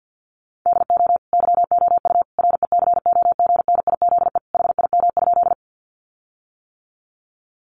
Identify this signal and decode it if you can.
Morse code, heard as a keyed tone, about 50 wpm, 700 Hz. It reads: DO YOU RECOGNIZE HIM?
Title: Sentences from Top 1400 Words 50wpm